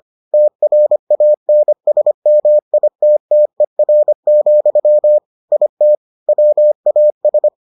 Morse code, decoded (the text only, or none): TRANSMITTER, IT WAS